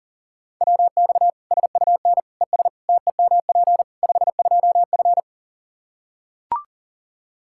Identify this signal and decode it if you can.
Morse code, heard as a keyed tone, about 40 wpm, 700 Hz. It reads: WX SUN ES TEMP 52F
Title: Practice QSO - Exchange 2 - Info - 40wpm